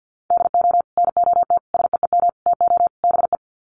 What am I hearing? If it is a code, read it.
Morse code, heard as a keyed tone, about 50 wpm, 700 Hz. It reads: DO NOT SEEM TO BE
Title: Sets of 5 Words Encore2 50wpm